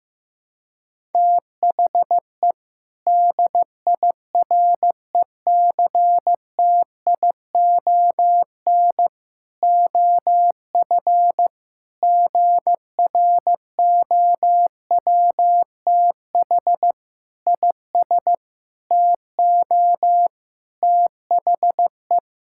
Morse code, heard as a keyed tone, about 16 wpm, 700 Hz. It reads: THE DIRECTION OF GROWTH IS TO THE
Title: Sentences from Top 1000 Words - Review Part 5 - 15wpm